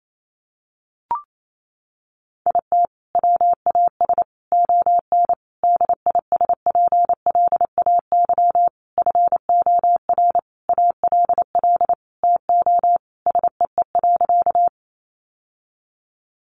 Morse code, heard as a keyed tone, about 28 wpm, 700 Hz. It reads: IT WAS ON DISPLAY FOR ALL TO SEE.
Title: Sentences from Top 1200 Words 28wpm